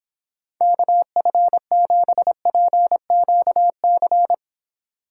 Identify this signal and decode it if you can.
Morse code, heard as a keyed tone, about 26 wpm, 700 Hz. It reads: KF7PQC